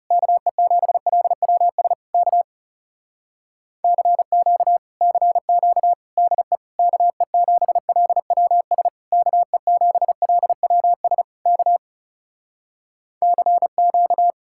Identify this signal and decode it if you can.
Morse code, heard as a keyed tone, about 37 wpm, 700 Hz. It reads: KE7LWS K CQ CQ DE KE7LWS KE7LWS K CQ
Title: Speed-Racing Common QSO Phrases - 30wpm